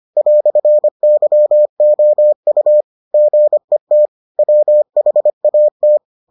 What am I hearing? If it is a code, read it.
Morse code, heard as a keyed tone, about 25 wpm, 600 Hz. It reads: "YOU GET WHAT